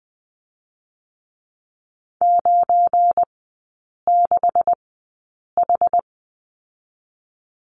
Morse code, heard as a keyed tone, about 20 wpm, spaced as for 8 wpm, 690 Hz. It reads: 96H
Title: 172 Lesson 28e - Number 9 - Three Characters 20wpm